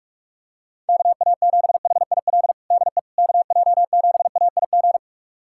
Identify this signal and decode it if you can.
Morse code, heard as a keyed tone, about 47 wpm, 700 Hz. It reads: KA7HIL DE KJ7AIG
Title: USA Callsign DE Callsign - 6 Characters - 45wpm